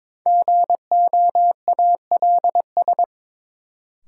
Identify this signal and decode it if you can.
Morse code, heard as a keyed tone, about 22 wpm, 700 Hz. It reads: GOALS
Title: Sentences from Top 1000 Words - Review Part 6 - New Word Drill - 22wpm